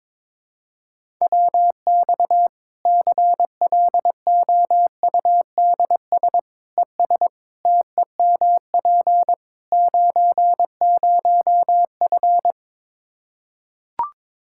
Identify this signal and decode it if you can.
Morse code, heard as a keyed tone, about 22 wpm, 700 Hz. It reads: WX CLOUDS ES TEMP 90F
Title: Practice QSO - Exchange 2 - Info - 22wpm